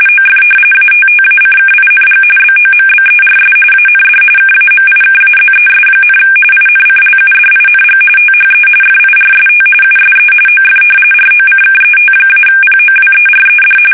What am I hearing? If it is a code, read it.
RTTY, about 45.45 baud. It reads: OF ATTACKING BOMBERS AND KAMIKAZES, INCLUDING THE FIRST USE OF ROCKET-POWERED
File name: fsk850 / battleshipnj